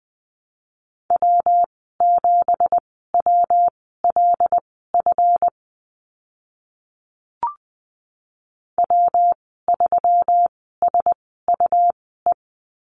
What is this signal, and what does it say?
Morse code, heard as a keyed tone, about 20 wpm, spaced as for 14 wpm, 700 Hz. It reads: W7WLF W3SUE
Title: 179 Lesson 29g - Slash - Call Signs 20wpm